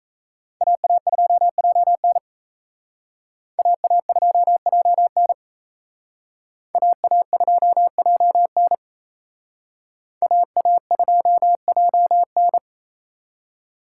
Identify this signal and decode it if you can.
Morse code, heard as a keyed tone, about 35 wpm, 700 Hz. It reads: AA2JN AA2JN AA2JN AA2JN
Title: Speed Racing - USA Call Signs 28wpm